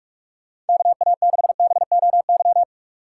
Morse code, heard as a keyed tone, about 54 wpm, 700 Hz. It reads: KA6BOY
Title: Speed Racing - USA Call Signs 45wpm